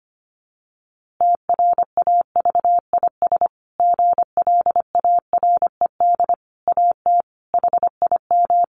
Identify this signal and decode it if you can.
Morse code, heard as a keyed tone, about 25 wpm, 700 Hz. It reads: TRAVIS GLARED AT HIM
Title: A Sound Of Thunder 25wpm